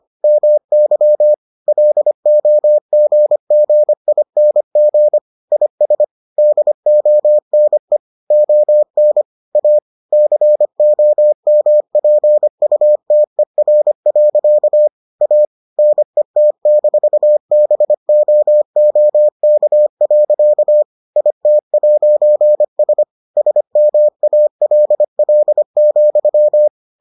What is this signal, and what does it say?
Morse code, heard as a keyed tone, about 25 wpm, 600 Hz. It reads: MY LOGGING IS DONE ON A COMPUTER. A NET-BOOK. IT'S SMALL,